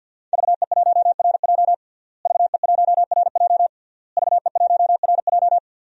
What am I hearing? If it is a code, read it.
Morse code, heard as a keyed tone, about 51 wpm, 700 Hz. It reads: VE1RJ VE1RJ VE1RJ
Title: DXCC-100 Call Signs - 5 Characters - Repeated 3x - 50wpm